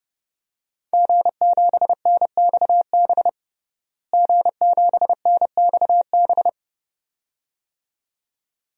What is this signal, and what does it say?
Morse code, heard as a keyed tone, about 30 wpm, 700 Hz. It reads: G7NXB G7NXB
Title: DXCC-100 Call Signs - 5 Characters - Repeated 2x - 30wpm